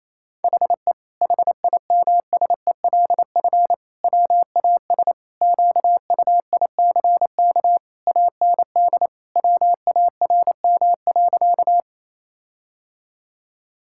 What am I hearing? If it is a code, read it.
Morse code, heard as a keyed tone, about 28 wpm, 700 Hz. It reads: HE HIMSELF WAS QUICK AND WARM.
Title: Sentences from Top 1600 Words 28wpm